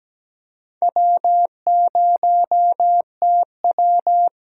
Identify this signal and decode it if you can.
Morse code, heard as a keyed tone, about 17 wpm, 700 Hz. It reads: W0TW
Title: USA Call Signs - 4 Characters - 6x Character Spacing - 17wpm